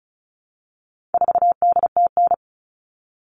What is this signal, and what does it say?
Morse code, heard as a keyed tone, about 34 wpm, 700 Hz. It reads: 4DTN
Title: All Cut Numbers - 4-Digit 35wpm